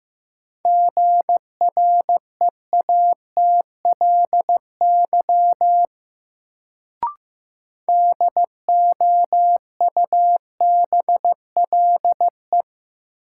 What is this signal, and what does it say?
Morse code, heard as a keyed tone, about 15 wpm, 700 Hz. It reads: GREATLY DOUBLE
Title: Sentences from Top 1400 Words - New Word Drill Alt - 15wpm